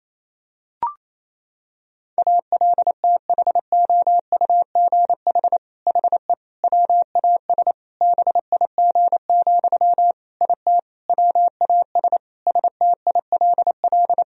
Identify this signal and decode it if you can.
Morse code, heard as a keyed tone, about 28 wpm, 700 Hz. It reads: ALTHOUGH HE WAS BIG, IT WAS STILL
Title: Sentences from Top 1000 Words - Review Part 3 - 28wpm